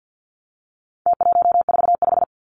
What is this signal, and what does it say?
Morse code, heard as a keyed tone, about 50 wpm, 700 Hz. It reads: T145
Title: Numbers - Contest Style - 4-Digit 50wpm